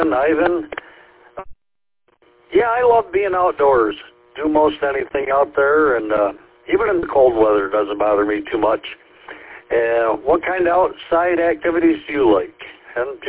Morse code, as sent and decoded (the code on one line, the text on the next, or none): none